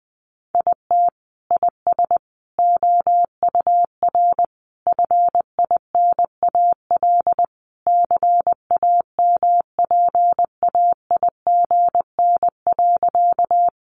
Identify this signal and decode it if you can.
Morse code, heard as a keyed tone, about 20 wpm, 700 Hz. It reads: IT IS OUR FINAL CAMPAIGN.
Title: Sentences from Top 1200 Words 20wpm